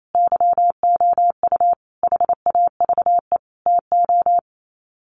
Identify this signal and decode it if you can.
Morse code, heard as a keyed tone, about 28 wpm, 700 Hz. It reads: YOU HAVE TO
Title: Sets of 5 Words 28wpm